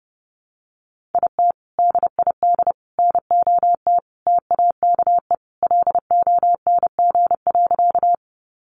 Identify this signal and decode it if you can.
Morse code, heard as a keyed tone, about 30 wpm, 700 Hz. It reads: IT DID NOT TAKE LONG.
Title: Sentences from Top 600 Words 30wpm